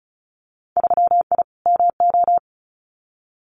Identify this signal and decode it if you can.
Morse code, heard as a keyed tone, about 34 wpm, 700 Hz. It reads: 3I MO
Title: Winter Field Day Exchanges 35wpm